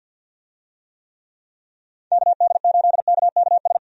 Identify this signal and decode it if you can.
Morse code, heard as a keyed tone, about 51 wpm, 700 Hz. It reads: KD8KKS
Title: USA Call Signs - 6 Characters - 50wpm